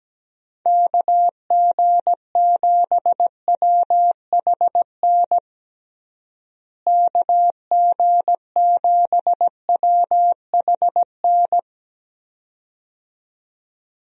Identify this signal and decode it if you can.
Morse code, heard as a keyed tone, about 17 wpm, 700 Hz. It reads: KG7WHN KG7WHN
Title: USA Call Signs - 6 Characters - Repeated 2x - 17wpm